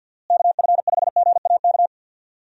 Morse code, heard as a keyed tone, about 52 wpm, 700 Hz. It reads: KF5GAK
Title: USA Call Signs - 6 Characters - Repeated 3x - 50wpm